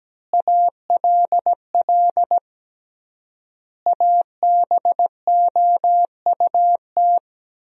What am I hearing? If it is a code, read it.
Morse code, heard as a keyed tone, about 17 wpm, 700 Hz. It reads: ALL ABOUT
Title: 3x Word-Spacing - Sentences from Top 200 Words - 17wpm